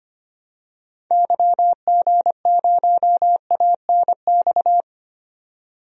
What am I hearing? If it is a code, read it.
Morse code, heard as a keyed tone, about 25 wpm, 700 Hz. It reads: YG0ANX
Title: DXCC-100 Call Signs - 6 Characters - 25wpm